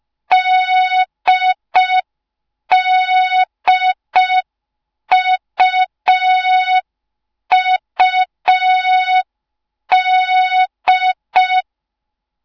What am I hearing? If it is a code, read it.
Morse code, heard as a keyed tone, about 5 wpm, 3700 Hz. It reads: DDUUD
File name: Morze-HA2MN / hang / 06BDU HA2MN